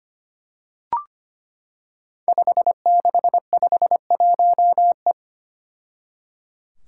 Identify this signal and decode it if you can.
Morse code, heard as a keyed tone, about 25 wpm, 700 Hz. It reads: 5651E
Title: Five Challenging Letter-Number Combinations 25wpm